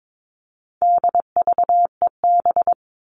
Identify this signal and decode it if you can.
Morse code, heard as a keyed tone, about 22 wpm, 700 Hz. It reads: DVEB
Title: All Cut Numbers - 4-Digit 22wpm